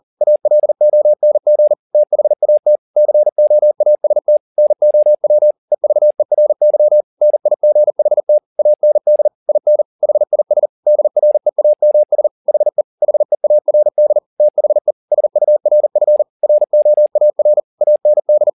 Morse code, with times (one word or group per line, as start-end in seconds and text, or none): 0.2-1.7 ALONG
1.9-2.8 THAT
3.0-4.4 COAST
4.6-5.5 NOW
5.7-7.0 EVERY
7.2-8.4 NIGHT
8.6-9.3 AND
9.5-9.8 IN
10.0-10.7 HIS
10.9-12.3 DREAMS
12.5-12.8 HE
13.0-14.2 HEARD
14.4-14.9 THE
15.1-16.2 SURF
16.4-17.6 ROAR
17.8-18.5 AND